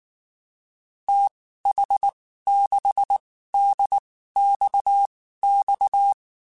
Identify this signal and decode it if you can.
Morse code, heard as a keyed tone, about 19 wpm, spaced as for 14 wpm, 800 Hz. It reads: TH6DXX
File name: (spaced) / QSOs Spaced-mp3s / WD0BGP 1 @18 sp